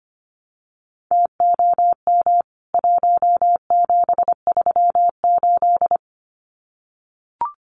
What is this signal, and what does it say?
Morse code, heard as a keyed tone, about 25 wpm, 700 Hz. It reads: TOM 1738
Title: CWT Exchange 25wpm